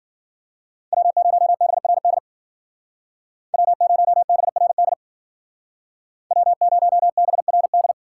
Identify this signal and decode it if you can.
Morse code, heard as a keyed tone, about 53 wpm, 700 Hz. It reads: W0BRD W0BRD W0BRD